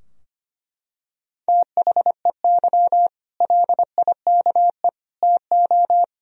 Morse code, heard as a keyed tone, about 25 wpm, 700 Hz. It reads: THEY LIKE TO
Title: Sentences from Top 1000 Words - Review Part 3 - 25wpm